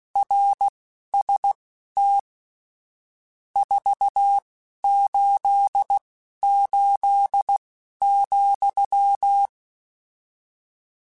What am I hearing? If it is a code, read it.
Morse code, heard as a keyed tone, about 16 wpm, spaced as for 11 wpm, 800 Hz. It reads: RST 488,